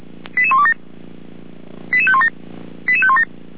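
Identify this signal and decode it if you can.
Five-tone selective call (ZVEI), heard as quick runs of five tones, tones of about 70 ms. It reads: 88127 88517 88517
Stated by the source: ZVEI five-tone call received by SDR.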